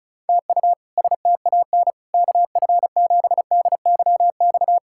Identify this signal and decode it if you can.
Morse code, heard as a keyed tone, about 36 wpm, 700 Hz. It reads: TU STAN KF7DYX
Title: K1USN SST QSO Practice - 35wpm